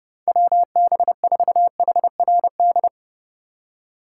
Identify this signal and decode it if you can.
Morse code, heard as a keyed tone, about 30 wpm, 700 Hz. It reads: WB4HRD
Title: USA Call Signs - 6 Characters - 3x Character Spacing - 30wpm